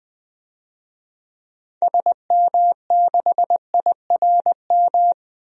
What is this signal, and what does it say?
Morse code, heard as a keyed tone, about 21 wpm, 700 Hz. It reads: SM6IRM
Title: DXCC-100 Call Signs - 6 Characters - 20wpm